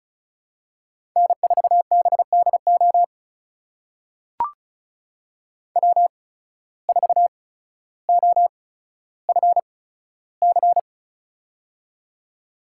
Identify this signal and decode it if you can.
Morse code, heard as a keyed tone, about 36 wpm, spaced as for 15 wpm, 700 Hz. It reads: N4BDO W4OFC
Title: USA Call Signs - 5 Characters - 8x Character Spacing - 35wpm